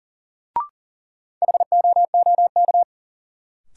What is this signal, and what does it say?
Morse code, heard as a keyed tone, about 40 wpm, 700 Hz. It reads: HOOK
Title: Sentences from Top 1600 Words - New Word Drill Alt - 40wpm